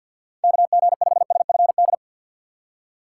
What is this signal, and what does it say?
Morse code, heard as a keyed tone, about 52 wpm, 700 Hz. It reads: KG5SFD
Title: USA Call Signs - 6 Characters - 6x Character Spacing - 50wpm